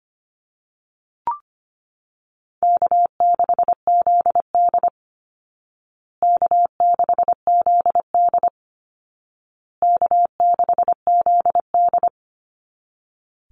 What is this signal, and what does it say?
Morse code, heard as a keyed tone, about 25 wpm, 700 Hz. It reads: K6ZD K6ZD K6ZD
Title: Call Signs Repeated 2x - 25wpm